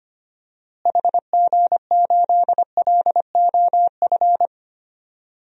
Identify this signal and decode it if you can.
Morse code, heard as a keyed tone, about 25 wpm, 700 Hz. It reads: HG8LOF
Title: DXCC-100 Call Signs - 6 Characters - Repeated 2x - 25wpm